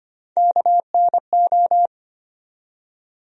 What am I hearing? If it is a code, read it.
Morse code, heard as a keyed tone, about 25 wpm, 700 Hz. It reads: KNO
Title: USA Call Sign Postfixes - 3 Letters 25wpm